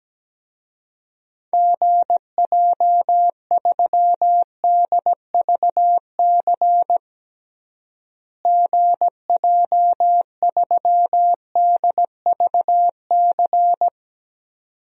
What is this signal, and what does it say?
Morse code, heard as a keyed tone, about 17 wpm, 700 Hz. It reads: GJ3DVC GJ3DVC
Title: DXCC-100 Call Signs - 6 Characters - Repeated 2x - 17wpm